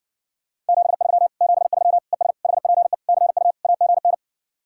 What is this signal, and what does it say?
Morse code, heard as a keyed tone, about 59 wpm, 700 Hz. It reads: 73 73 ES HPE CU AGN